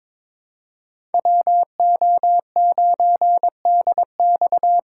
Morse code, heard as a keyed tone, about 21 wpm, 700 Hz. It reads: WO9DX
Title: USA Call Signs - 5 Characters - 3x Character Spacing - 22wpm